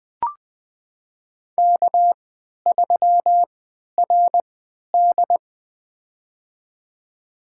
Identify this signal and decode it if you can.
Morse code, heard as a keyed tone, about 20 wpm, spaced as for 11 wpm, 700 Hz. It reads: K3RD
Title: USA Call Signs - 4 Characters - 3x Character Spacing - 20wpm